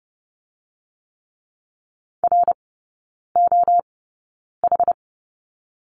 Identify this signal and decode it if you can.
Morse code, heard as a keyed tone, about 30 wpm, spaced as for 9 wpm, 690 Hz. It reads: ROH
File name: Lesson 16e - Letter c - Three Characters 30wpm